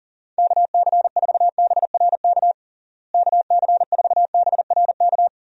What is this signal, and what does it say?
Morse code, heard as a keyed tone, about 40 wpm, 700 Hz. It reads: KC4BRK KC4BRK
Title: USA Call Signs - 6 Characters - Repeated 2x - 40wpm